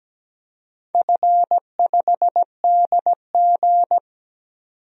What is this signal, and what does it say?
Morse code, heard as a keyed tone, about 17 wpm, 700 Hz. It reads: F5DG
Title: DXCC-100 Call Signs - 4 Characters - 17wpm